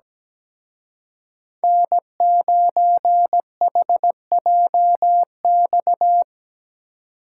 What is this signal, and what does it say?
Morse code, heard as a keyed tone, about 20 wpm, 700 Hz. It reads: N9HJX
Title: Speed Racing - USA Call Signs 17wpm